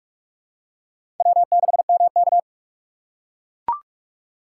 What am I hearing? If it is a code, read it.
Morse code, heard as a keyed tone, about 44 wpm, 700 Hz. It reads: W6MK
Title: USA Call Signs - 4 Characters - 6x Character Spacing - 45wpm